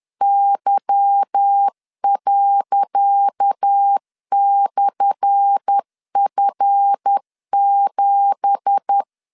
Morse code, heard as a keyed tone, about 10 wpm, 800 Hz. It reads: Y./F7
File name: LEZIONI / 10 SIGNS / 10C DOT 10wpm 2940